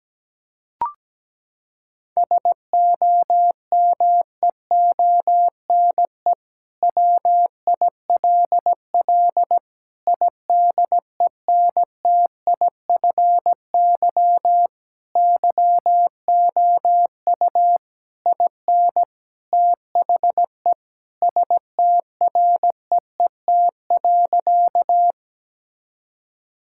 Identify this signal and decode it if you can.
Morse code, heard as a keyed tone, about 17 wpm, 700 Hz. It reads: SOMEONE WILL IDENTIFY YOU IN THE STREET.
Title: Sentences from Top 1000 Words - Review Part 6 - 17wpm